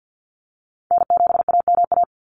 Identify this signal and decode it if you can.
Morse code, heard as a keyed tone, about 49 wpm, 700 Hz. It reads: N7AMA
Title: NA SOTA Chaser Callsigns 50wpm